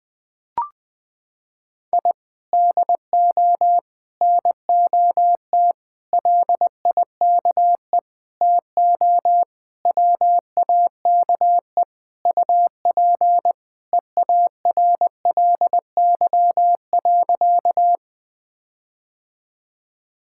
Morse code, heard as a keyed tone, about 20 wpm, 700 Hz. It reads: I DO NOT LIKE TO WAKE UP EARLY.
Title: Sentences with 5-Word Phrases - 20wpm